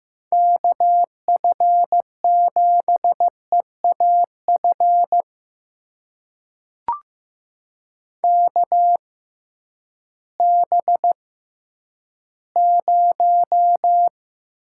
Morse code, heard as a keyed tone, about 15 wpm, spaced as for 10 wpm, 700 Hz. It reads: KF7EAF KB0
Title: USA Call Signs - 6 Characters - 6x Character Spacing - 15wpm